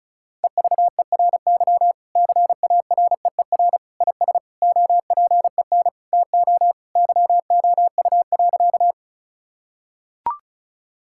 Morse code, heard as a keyed tone, about 36 wpm, 700 Hz. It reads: EVERY CAREER IS OPEN TO YOU.
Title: Sentences from Top 1000 Words - Review Part 6 - 35wpm